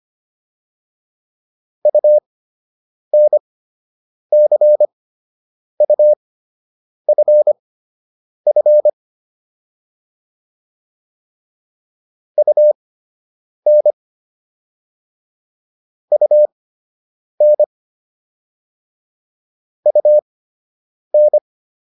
Morse code, heard as a keyed tone, about 25 wpm, spaced as for 8 wpm, 600 Hz. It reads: UNCUFF UN UN UN